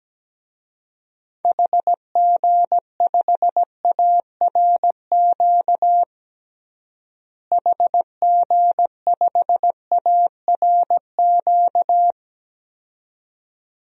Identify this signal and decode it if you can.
Morse code, heard as a keyed tone, about 17 wpm, 700 Hz. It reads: HG5ARQ HG5ARQ
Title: DXCC-100 Call Signs - 6 Characters - Repeated 2x - 17wpm